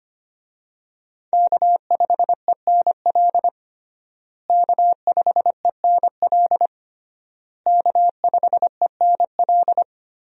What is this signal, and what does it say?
Morse code, heard as a keyed tone, about 25 wpm, 700 Hz. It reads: K5ENL K5ENL K5ENL
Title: USA Call Signs - 5 Characters - Repeated 3x - 25wpm